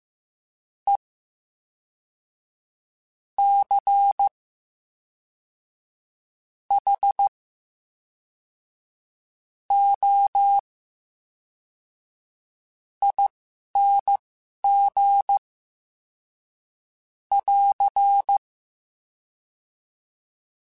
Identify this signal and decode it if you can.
Morse code, heard as a keyed tone, about 15 wpm, spaced as for 11 wpm, 800 Hz. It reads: E C H O ING <AR>
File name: Special INGs / Special INGs-mp3s / VerySlowToFaster Ing @10